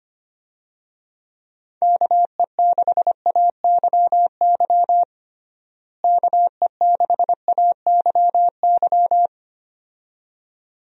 Morse code, heard as a keyed tone, about 25 wpm, 700 Hz. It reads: KE6AYY KE6AYY
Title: USA Call Signs - 6 Characters - Repeated 2x - 25wpm